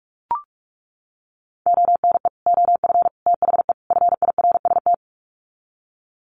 Morse code, heard as a keyed tone, about 44 wpm, 700 Hz. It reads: ONE OF THE FIRST
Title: Sets of 4 Words 45wpm